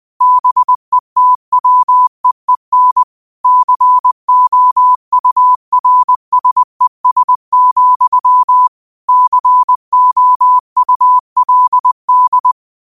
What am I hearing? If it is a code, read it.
Morse code, heard as a keyed tone, about 20 wpm, 1000 Hz. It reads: BETWEEN COURSES, COULD
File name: qotdfiles / 20wpm